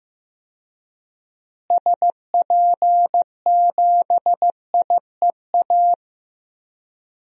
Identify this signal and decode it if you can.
Morse code, heard as a keyed tone, about 16 wpm, 700 Hz. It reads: SP7IEA
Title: DXCC-100 Call Signs - 6 Characters - Repeated 3x - 15wpm